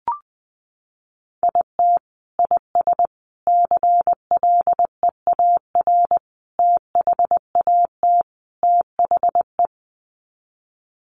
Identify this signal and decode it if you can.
Morse code, heard as a keyed tone, about 19 wpm, 700 Hz. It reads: IT IS CLEAR THAT THE
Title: Sets of 5 Words Encore 20wpm